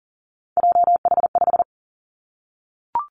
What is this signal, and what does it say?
Morse code, heard as a keyed tone, about 42 wpm, 700 Hz. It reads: JH5
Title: Three Challenging Letter-Number Combinations 40wpm